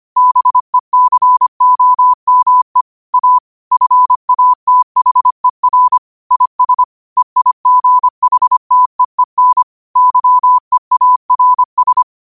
Morse code, heard as a keyed tone, about 25 wpm, 1000 Hz. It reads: BECOME A FATHER IS EIGHTEEN YEARS
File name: qotdfiles / 25wpm